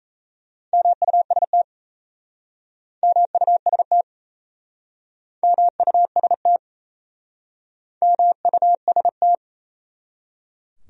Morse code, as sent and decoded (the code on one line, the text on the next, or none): -- ..- ... - / -- ..- ... - / -- ..- ... - / -- ..- ... -
MUST MUST MUST MUST